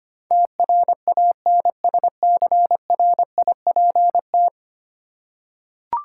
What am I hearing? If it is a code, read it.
Morse code, heard as a keyed tone, about 25 wpm, 690 Hz. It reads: TRANSCRIPT
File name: 10 Letter Words - First 3 Letters Given - 25wpm